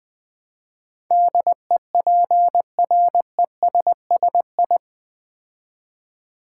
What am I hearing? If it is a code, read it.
Morse code, heard as a keyed tone, about 20 wpm, 700 Hz. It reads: DEPRESSI
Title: Build A Word - 4 Letters At A Time - 10-Letter Words - 20wpm